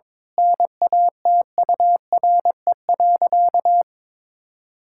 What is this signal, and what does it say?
Morse code, heard as a keyed tone, about 22 wpm, 700 Hz. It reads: NATURE.
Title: Sentences from Top 1000 Words - Review Part 5 - 22wpm